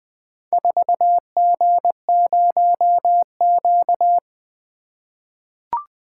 Morse code, heard as a keyed tone, about 20 wpm, 700 Hz. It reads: 4G0Q
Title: Four Letter-Number Combinations - 20wpm